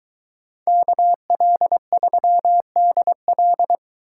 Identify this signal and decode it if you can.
Morse code, heard as a keyed tone, about 23 wpm, 700 Hz. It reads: KL3DL